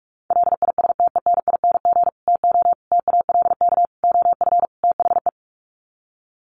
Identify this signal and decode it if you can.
Morse code, heard as a keyed tone, about 44 wpm, 700 Hz. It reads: LISTENING TO TALK OF THE
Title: Sets of 5 Words 45wpm